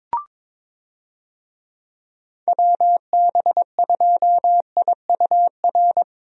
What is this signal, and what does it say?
Morse code, heard as a keyed tone, about 22 wpm, 700 Hz. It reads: WB2IUR
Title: USA Call Signs - 6 Characters - 22wpm